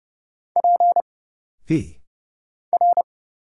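Morse code, heard as a keyed tone, about 30 wpm, 700 Hz. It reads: P R
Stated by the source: Morse Code Ninja single-character practice (letters, numbers, punctuation, prosigns) at 30 wpm, by Kurt Zoglmann (AD0WE).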